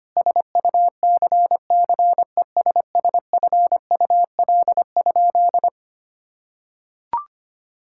Morse code, heard as a keyed tone, about 25 wpm, 700 Hz. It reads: SUCCESSFUL?
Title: Sentences from Top 1000 Words - Review Part 6 - 25wpm